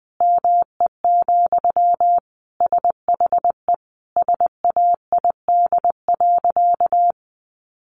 Morse code, heard as a keyed tone, about 20 wpm, 700 Hz. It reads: ME, SHE SAID.
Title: Sentences from Top 500 Words 20wpm